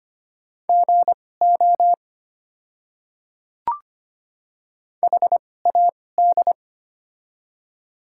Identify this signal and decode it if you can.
Morse code, heard as a keyed tone, about 25 wpm, spaced as for 18 wpm, 700 Hz. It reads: GO HAD
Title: 2x Character Spacing - Top 100 Words - 25wpm